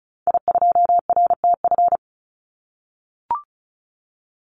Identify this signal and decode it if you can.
Morse code, heard as a keyed tone, about 34 wpm, 700 Hz. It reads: I2RTF
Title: Call Signs - 35wpm